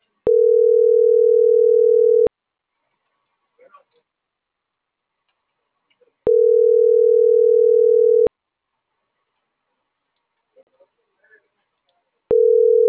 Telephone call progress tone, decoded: ringback tone

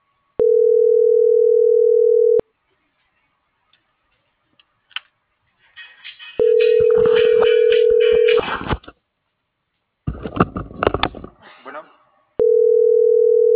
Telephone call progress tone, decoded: ringback tone

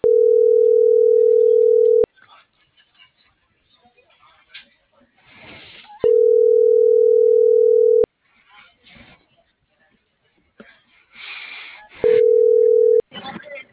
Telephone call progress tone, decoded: ringback tone

